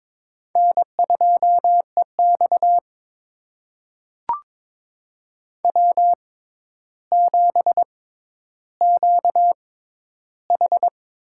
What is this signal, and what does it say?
Morse code, heard as a keyed tone, about 22 wpm, spaced as for 11 wpm, 700 Hz. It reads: N2EX W7QH
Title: USA Call Signs - 4 Characters - 6x Character Spacing - 22wpm